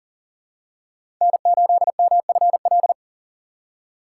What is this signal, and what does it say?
Morse code, heard as a keyed tone, about 40 wpm, 700 Hz. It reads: N8MFL
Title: USA Call Signs - 3x Character Spacing - 40wpm